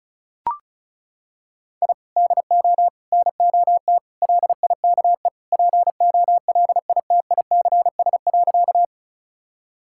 Morse code, heard as a keyed tone, about 34 wpm, 700 Hz. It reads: I DO NOT LIKE POLITICS.
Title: Sentences from Top 1000 Words - Review Part 6 - 35wpm